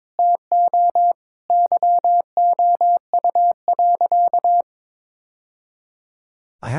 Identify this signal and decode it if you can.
Morse code, heard as a keyed tone, about 22 wpm, 700 Hz. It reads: TO YOU.
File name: practice / 22wpm / Sentences with 3-Word Phrases - 22wpm